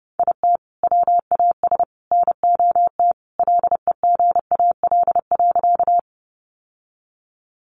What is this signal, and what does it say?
Morse code, heard as a keyed tone, about 30 wpm, 700 Hz. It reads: IT WAS NOT LEGAL.
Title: Sentences from Top 1200 Words 30wpm